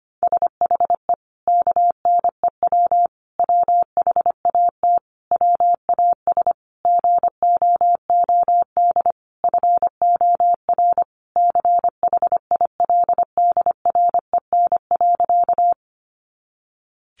Morse code, heard as a keyed tone, about 25 wpm, 700 Hz. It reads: SHE KNEW WHAT WAS GOOD FOR CHILDREN.
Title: Sentences from Top 400 Words 25wpm